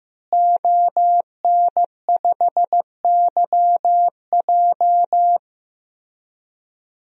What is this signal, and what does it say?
Morse code, heard as a keyed tone, about 15 wpm, 700 Hz. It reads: ON5YJ
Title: DXCC-100 Call Signs - 5 Characters - Repeated 3x - 15wpm